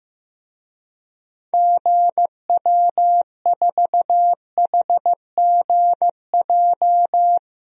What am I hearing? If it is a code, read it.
Morse code, heard as a keyed tone, about 15 wpm, 700 Hz. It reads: GW4HGJ
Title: DXCC-100 Call Signs - 6 Characters - Repeated 2x - 15wpm